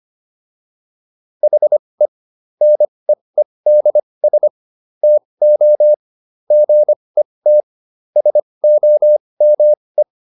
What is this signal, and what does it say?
Morse code, heard as a keyed tone, about 25 wpm, spaced as for 20 wpm, 600 Hz. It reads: HE NEEDS TO GET SOME